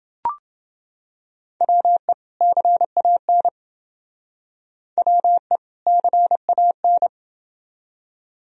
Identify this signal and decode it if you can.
Morse code, heard as a keyed tone, about 30 wpm, 700 Hz. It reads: WE CAN WE CAN